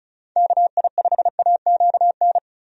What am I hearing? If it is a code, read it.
Morse code, heard as a keyed tone, about 36 wpm, 700 Hz. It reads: KI5AQN
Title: USA Call Signs - 6 Characters - Repeated 2x - 35wpm